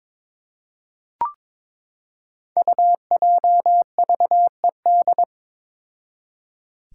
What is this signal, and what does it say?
Morse code, heard as a keyed tone, about 22 wpm, 700 Hz. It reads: UJVED
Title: Five Challenging Letter-Number Combinations 22wpm